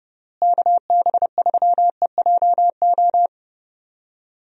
Morse code, heard as a keyed tone, about 32 wpm, 700 Hz. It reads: KB3EJO
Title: USA Call Signs - 6 Characters - Repeated 3x - 30wpm